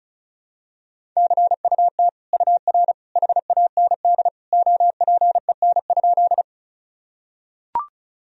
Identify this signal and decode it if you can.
Morse code, heard as a keyed tone, about 36 wpm, 700 Hz. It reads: CUT UR HAND OPEN?
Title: Sentences from Top 300 Words and Abbreviations 35wpm